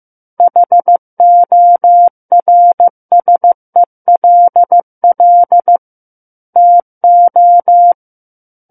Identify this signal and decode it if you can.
Morse code, heard as a keyed tone, about 15 wpm, 700 Hz. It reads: HORSELL TO
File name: Book One / 06-Chapter 6 - 15 WPM